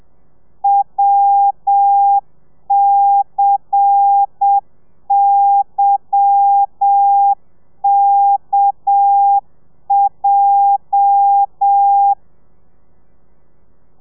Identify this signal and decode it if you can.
Morse code, heard as a keyed tone, about 7 wpm, 800 Hz. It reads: WCYKJ
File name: telegrafia / i2tfj / Lez10 30